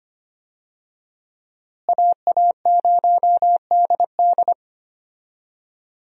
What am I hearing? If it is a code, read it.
Morse code, heard as a keyed tone, about 25 wpm, 700 Hz. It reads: AA0DD
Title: USA Call Signs - 5 Characters - Repeated 3x - 25wpm